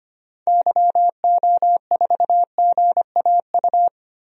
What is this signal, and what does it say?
Morse code, heard as a keyed tone, about 25 wpm, 690 Hz. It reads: YO4GAU
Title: DXCC-100 Call Signs - 6 Characters - Repeated 3x - 25wpm